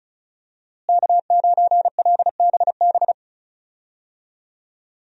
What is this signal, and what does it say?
Morse code, heard as a keyed tone, about 36 wpm, 700 Hz. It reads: K9LBB
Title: USA Call Signs - 5 Characters - Repeated 2x - 35wpm